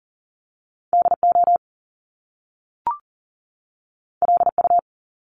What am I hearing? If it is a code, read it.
Morse code, heard as a keyed tone, about 38 wpm, 700 Hz. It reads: DO LU
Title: Top Two Letter Combinations in Words 40wpm